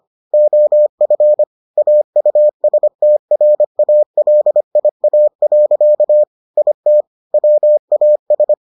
Morse code, heard as a keyed tone, about 25 wpm, 600 Hz. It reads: OF AUSTRALIA. IT WAS